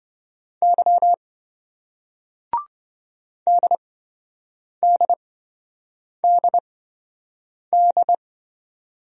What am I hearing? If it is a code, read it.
Morse code, heard as a keyed tone, about 28 wpm, spaced as for 7 wpm, 700 Hz. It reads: Y DDDD